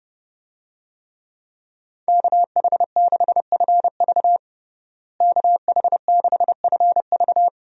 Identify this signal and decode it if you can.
Morse code, heard as a keyed tone, about 30 wpm, 700 Hz. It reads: KH6FV KH6FV
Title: USA Call Signs - 5 Characters - Repeated 2x - 30wpm